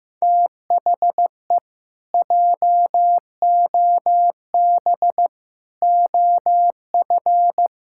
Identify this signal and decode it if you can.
Morse code, heard as a keyed tone, about 15 wpm, 700 Hz. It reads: THE JOB OF
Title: Pangram Challenge v2 - 15wpm